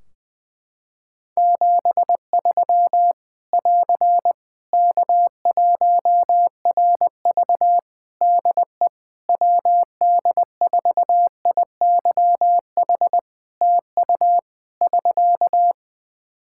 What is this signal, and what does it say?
Morse code, heard as a keyed tone, about 21 wpm, 700 Hz. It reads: 73 <AR> K1RV DE WD4IYH TU <SK>
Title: Practice QSO - Exchange 4 - Ending - 20wpm